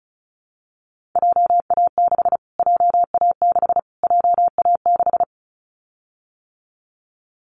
Morse code, heard as a keyed tone, about 34 wpm, 700 Hz. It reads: JA6 JA6 JA6
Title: Callsign Prefixes for Top 25 DXCC - Repeated 3x - 35wpm